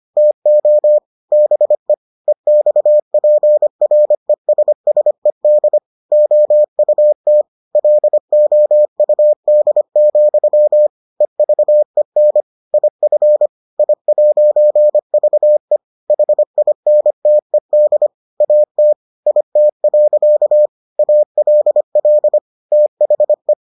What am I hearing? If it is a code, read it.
Morse code, heard as a keyed tone, about 25 wpm, 600 Hz. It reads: TO BE EXPRESSED OUT LOUD, EVEN IF I'VE HINTED AT IT. ALL THE